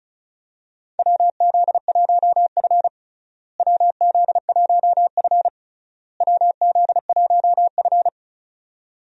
Morse code, heard as a keyed tone, about 36 wpm, 700 Hz. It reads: WZ1F WZ1F WZ1F
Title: USA Call Signs - 4 Characters - Repeated 3x - 35wpm